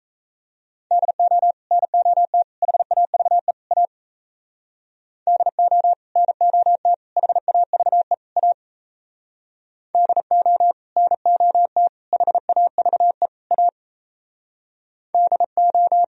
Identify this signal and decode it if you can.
Morse code, heard as a keyed tone, about 36 wpm, 700 Hz. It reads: DO NOT HAVE A DO NOT HAVE A DO NOT HAVE A DO